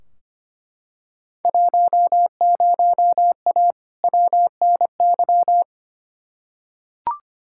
Morse code, heard as a keyed tone, about 25 wpm, 700 Hz. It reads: 10A WNY
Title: Field Day Exchanges - 25wpm